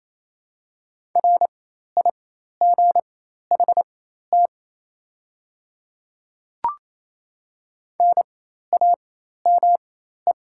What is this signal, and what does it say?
Morse code, heard as a keyed tone, about 28 wpm, spaced as for 13 wpm, 690 Hz. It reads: RIGHT NAME
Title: 4x Character Spacing - Top 200 Words - 28wpm